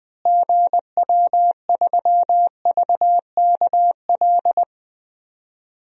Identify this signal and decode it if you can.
Morse code, heard as a keyed tone, about 20 wpm, 700 Hz. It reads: GW3VKL
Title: DXCC-100 Call Signs - 6 Characters - Repeated 3x - 20wpm